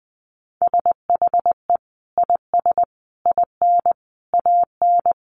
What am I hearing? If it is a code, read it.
Morse code, heard as a keyed tone, about 20 wpm, 700 Hz. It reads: SHE IS IN AN
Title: Sentences from Top 1000 Words 20wpm